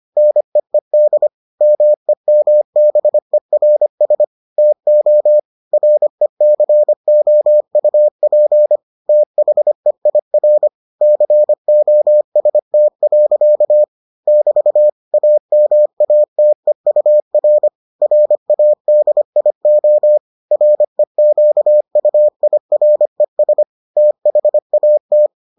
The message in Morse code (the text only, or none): NEED MEMBERS TO RECOUP THEIR COST. <BT> AMATEUR RADIO REQUIRES THAT